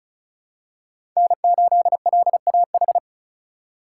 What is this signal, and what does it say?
Morse code, heard as a keyed tone, about 34 wpm, 700 Hz. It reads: N8LAH
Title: USA Call Signs - 5 Characters - 2x Character Spacing - 35wpm